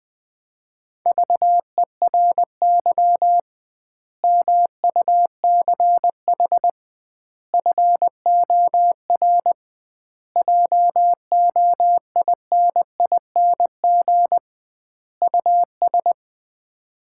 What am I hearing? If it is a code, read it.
Morse code, heard as a keyed tone, about 20 wpm, 700 Hz. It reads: VERY MUCH FOR JOINING US
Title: 2x Word-Spacing - Sets of 5 Words - 20wpm